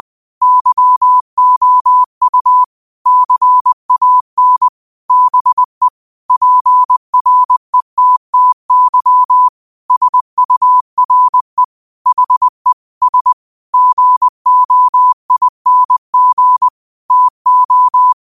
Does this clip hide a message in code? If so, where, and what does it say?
Yes: Morse code from 0.4 to 18.1 s, heard as a keyed tone, about 20 wpm, 1000 Hz. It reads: YOU CAN BE PRETTY SURE HE S GOING TO